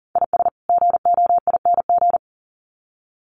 Morse code, heard as a keyed tone, about 40 wpm, 700 Hz. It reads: IS GOING
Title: Sets of 2 Words 40wpm